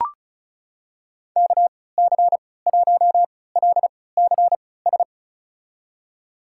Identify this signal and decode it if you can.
Morse code, heard as a keyed tone, about 36 wpm, spaced as for 20 wpm, 700 Hz. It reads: KC1LCS
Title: USA Call Signs - 6 Characters - 3x Character Spacing - 35wpm